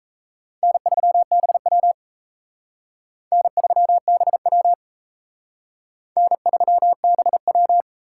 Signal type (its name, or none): Morse code